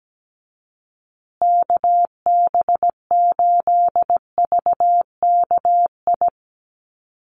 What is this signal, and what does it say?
Morse code, heard as a keyed tone, about 17 wpm, 700 Hz. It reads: KB8VKI
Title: USA Call Signs Repeated 3x 17wpm